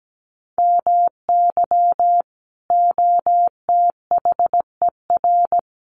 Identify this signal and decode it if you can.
Morse code, heard as a keyed tone, about 17 wpm, 700 Hz. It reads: MY OTHER
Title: Sentences from Top 1300 Words 17wpm